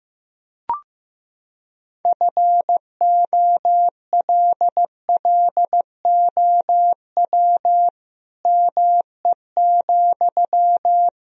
Morse code, heard as a keyed tone, about 15 wpm, 700 Hz. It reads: FOLLOW ME,
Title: Sentences from Top 1000 Words - Review Part 3 - 15wpm